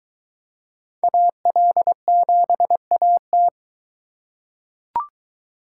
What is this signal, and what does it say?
Morse code, heard as a keyed tone, about 22 wpm, 700 Hz. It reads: AL7AT